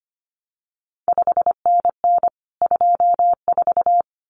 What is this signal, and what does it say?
Morse code, heard as a keyed tone, about 25 wpm, 700 Hz. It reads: 5NN 24
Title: CQ WW DX Contest Exchange Encore - 25wpm